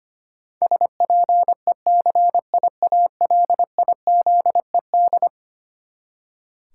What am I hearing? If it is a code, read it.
Morse code, heard as a keyed tone, about 25 wpm, 700 Hz. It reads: SPECIALIZED
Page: morse-code-ninja-practice-25wpm